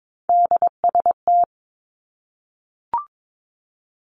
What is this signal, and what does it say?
Morse code, heard as a keyed tone, about 22 wpm, 700 Hz. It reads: DST
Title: Three Challenging Letter-Number Combinations 22wpm